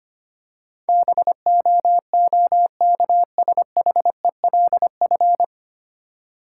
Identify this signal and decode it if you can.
Morse code, heard as a keyed tone, about 25 wpm, 700 Hz. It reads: BOOKSHELF
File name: practice / 25wpm / Difficult Words - Speed-Racing with Hints - 25wpm